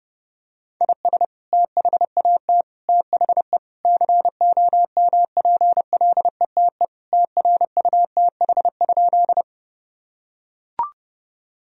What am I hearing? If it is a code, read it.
Morse code, heard as a keyed tone, about 30 wpm, 700 Hz. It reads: IS THAT THE COMPLETE TRUTH?
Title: Sentences from Top 1000 Words - Review Part 2 - 30wpm